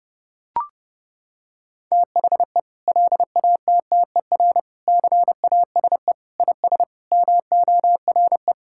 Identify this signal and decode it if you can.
Morse code, heard as a keyed tone, about 30 wpm, 700 Hz. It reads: THE LATTER CASE IS MORE
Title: Sentences from Top 1600 Words 30wpm